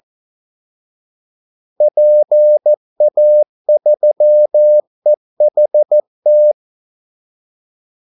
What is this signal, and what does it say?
Morse code, heard as a keyed tone, about 14 wpm, 600 Hz. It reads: PA3EHT
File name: Callsigns-14wpm